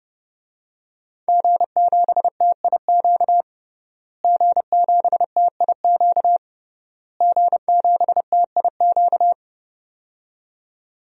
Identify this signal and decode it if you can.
Morse code, heard as a keyed tone, about 29 wpm, 700 Hz. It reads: G7TIQ G7TIQ G7TIQ
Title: DXCC-100 Call Signs - 5 Characters - Repeated 3x - 30wpm